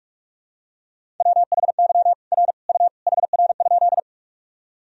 Morse code, heard as a keyed tone, about 46 wpm, 700 Hz. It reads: WHY R U HR?
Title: Sentences from Top 200 Words and Abbreviations 45wpm